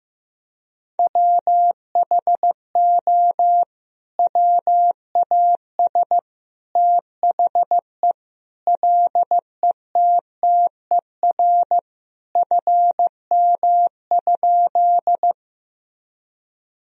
Morse code, heard as a keyed tone, about 15 wpm, 700 Hz. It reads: WHO WAS THE LETTER FM?
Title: Sentences from Top 300 Words and Abbreviations 15wpm